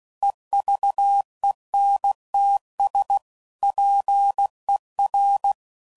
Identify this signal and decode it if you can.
Morse code, heard as a keyed tone, about 16 wpm, 800 Hz. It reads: EVENTS PER
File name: CW Code Groups @15WPM / Technical Subjects / XC